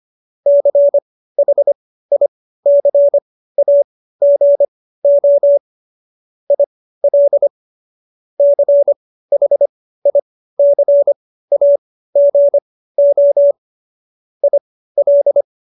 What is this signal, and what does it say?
Morse code, heard as a keyed tone, about 25 wpm, spaced as for 15 wpm, 570 Hz. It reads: CHICAGO IL CHICAGO IL
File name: qso222 15